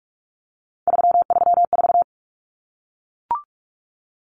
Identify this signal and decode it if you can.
Morse code, heard as a keyed tone, about 44 wpm, 700 Hz. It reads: 334
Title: Numbers - 3-Digit 45wpm